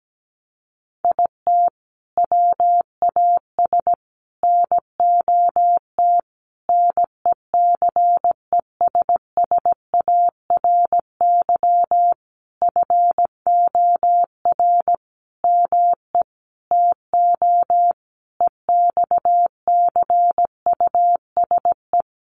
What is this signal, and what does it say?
Morse code, heard as a keyed tone, about 17 wpm, 700 Hz. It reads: IT WAS NOT NECESSARY FOR ME TO EXCUSE
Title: Sentences from Top 1300 Words 17wpm